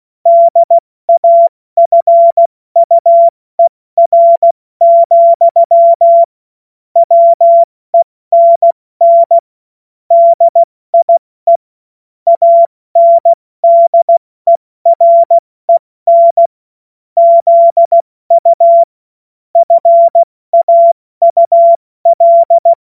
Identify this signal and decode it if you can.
Morse code, heard as a keyed tone, about 16 wpm, spaced as for 14 wpm, 680 Hz. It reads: DAFUER, WENN DIE ANDEREN ZU FAUL
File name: Witz-07-Oktober-2025-70-F80